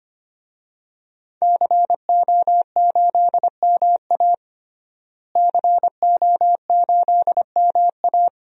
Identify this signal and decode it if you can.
Morse code, heard as a keyed tone, about 25 wpm, 700 Hz. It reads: CO8MA CO8MA
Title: DXCC-100 Call Signs - 5 Characters - Repeated 2x - 25wpm